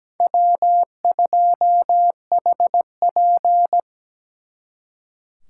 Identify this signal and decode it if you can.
Morse code, heard as a keyed tone, about 20 wpm, 700 Hz. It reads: W2HP
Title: Speed Racing - USA Call Signs 17wpm